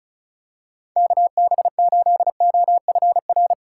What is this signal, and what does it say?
Morse code, heard as a keyed tone, about 36 wpm, 700 Hz. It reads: KB8OFR
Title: USA Call Signs - 6x Character Spacing - 35wpm